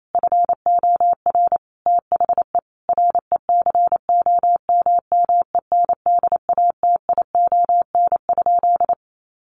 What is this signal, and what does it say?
Morse code, heard as a keyed tone, about 28 wpm, 700 Hz. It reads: FOR THE RECOMMENDATION?
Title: Sentences from Top 1400 Words 28wpm